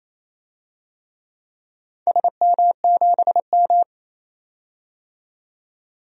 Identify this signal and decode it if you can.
Morse code, heard as a keyed tone, about 28 wpm, 700 Hz. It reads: SM7M
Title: DXCC-100 Call Signs - 4 Characters - 28wpm